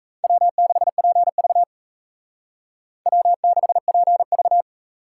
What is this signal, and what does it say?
Morse code, heard as a keyed tone, about 34 wpm, 700 Hz. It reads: W6PV W6PV
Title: Speed Racing - USA 1x2 Call Signs 28wpm